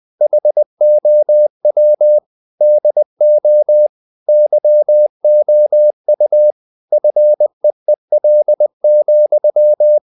Morse code, heard as a keyed tone, about 20 wpm, 600 Hz. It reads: HOW DO YOU FEEL,
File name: the old man and the sea part 2 20wpm